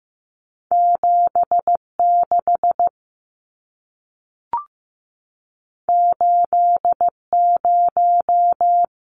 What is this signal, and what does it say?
Morse code, heard as a keyed tone, about 15 wpm, 700 Hz. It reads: 7 6 8 0
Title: Numbers - 2-Digit 15wpm